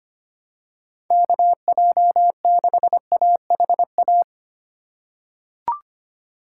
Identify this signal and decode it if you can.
Morse code, heard as a keyed tone, about 25 wpm, 700 Hz. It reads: KJ6AHA
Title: USA Call Signs - 2x Character Spacing - 25wpm